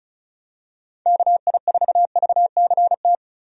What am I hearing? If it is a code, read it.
Morse code, heard as a keyed tone, about 34 wpm, 700 Hz. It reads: KI4VCT
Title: USA Call Signs - 6 Characters - Repeated 3x - 35wpm